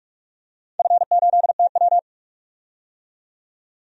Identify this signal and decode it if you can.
Morse code, heard as a keyed tone, about 46 wpm, 700 Hz. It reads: F8TW
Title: DXCC-100 Call Signs - 4 Characters - 45wpm